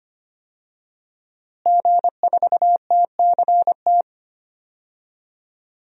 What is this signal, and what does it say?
Morse code, heard as a keyed tone, about 25 wpm, 700 Hz. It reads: G4TCT
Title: DXCC-100 Call Signs - 5 Characters - Repeated 2x - 25wpm